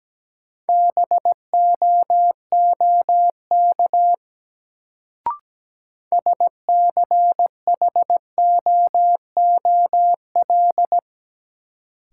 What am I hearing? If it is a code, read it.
Morse code, heard as a keyed tone, about 17 wpm, 700 Hz. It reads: BOOK SCHOOL
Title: Sentences from Top 300 Words - New Word Drill - 17wpm